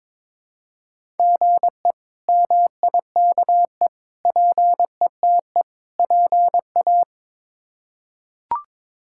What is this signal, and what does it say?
Morse code, heard as a keyed tone, about 22 wpm, 700 Hz. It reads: GE MIKE PETE PA
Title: K1USN SST QSO Practice - 22wpm